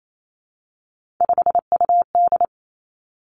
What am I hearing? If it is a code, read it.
Morse code, heard as a keyed tone, about 28 wpm, 700 Hz. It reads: 5UD
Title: Three Challenging Letter-Number Combinations 28wpm